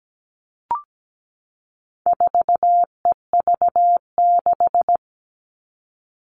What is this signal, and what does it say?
Morse code, heard as a keyed tone, about 17 wpm, 700 Hz. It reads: 4EV6
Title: All Cut Numbers - 4-Digit 17wpm